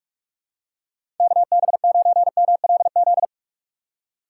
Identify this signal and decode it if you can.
Morse code, heard as a keyed tone, about 46 wpm, 700 Hz. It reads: KB9MLZ
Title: USA Call Signs - 6 Characters - 3x Character Spacing - 45wpm